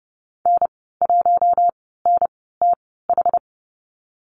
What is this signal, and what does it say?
Morse code, heard as a keyed tone, about 32 wpm, spaced as for 17 wpm, 700 Hz. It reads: N1NTH
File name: Lesson 37g - Letter x - Call Signs 30wpm